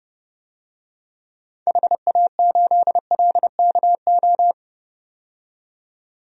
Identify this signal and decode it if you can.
Morse code, heard as a keyed tone, about 30 wpm, 700 Hz. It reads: HA8LKO
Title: DXCC-100 Call Signs - 6 Characters - 30wpm